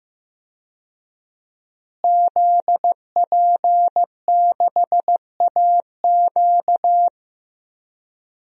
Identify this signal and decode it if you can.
Morse code, heard as a keyed tone, about 15 wpm, 700 Hz. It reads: ZP6AQ
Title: DXCC-100 Call Signs - 5 Characters - Repeated 2x - 15wpm